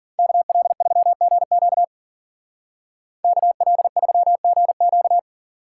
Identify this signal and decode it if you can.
Morse code, heard as a keyed tone, about 53 wpm, 700 Hz. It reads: KL3GQ KL3GQ